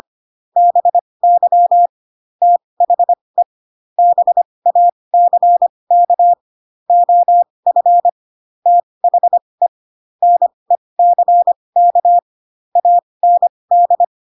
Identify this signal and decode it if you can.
Morse code, heard as a keyed tone, about 25 wpm, spaced as for 20 wpm, 700 Hz. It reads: BY THE BACK OF THE NECK AND